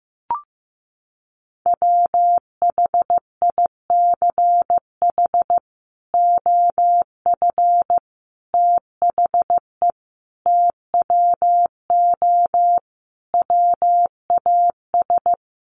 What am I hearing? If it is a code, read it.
Morse code, heard as a keyed tone, about 15 wpm, 700 Hz. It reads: WHICH OF THE TWO WAS
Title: Sentences from Top 1300 Words 15wpm